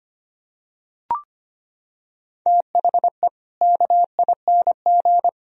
Morse code, heard as a keyed tone, about 25 wpm, 700 Hz. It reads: THE KING
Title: Sentences from Top 1000 Words - Review Part 3 - 25wpm